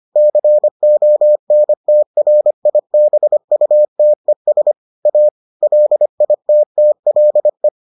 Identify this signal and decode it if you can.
Morse code, heard as a keyed tone, about 25 wpm, 600 Hz. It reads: CONTRIBUTES A LITTLE